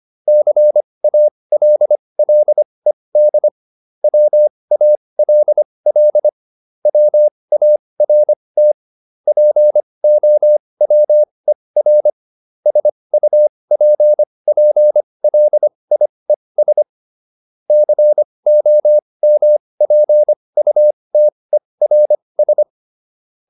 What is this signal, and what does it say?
Morse code, heard as a keyed tone, about 25 wpm, spaced as for 20 wpm, 600 Hz. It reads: CALLED WALL WART POWER SUPPLIES COMPUTERS